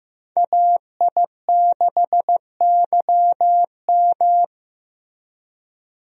Morse code, heard as a keyed tone, about 15 wpm, 700 Hz. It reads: AI6YM
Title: USA Call Signs - 5 Characters - Repeated 3x - 15wpm